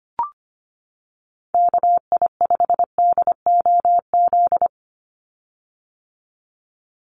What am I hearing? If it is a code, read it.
Morse code, heard as a keyed tone, about 25 wpm, 700 Hz. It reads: KI5DOZ
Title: USA Call Signs 25wpm